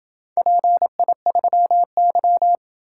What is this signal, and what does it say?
Morse code, heard as a keyed tone, about 27 wpm, 700 Hz. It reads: PI3Y